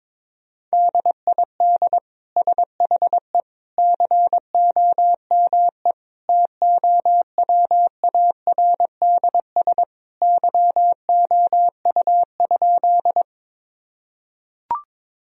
Morse code, heard as a keyed tone, about 22 wpm, 700 Hz. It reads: DID SHE COME TOWARDS YOU?
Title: Sentences from Top 1000 Words - Review Part 2 - 22wpm